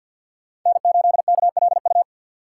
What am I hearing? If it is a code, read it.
Morse code, heard as a keyed tone, about 53 wpm, 700 Hz. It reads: N8KLU
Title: USA Call Signs - 5 Characters - 6x Character Spacing - 50wpm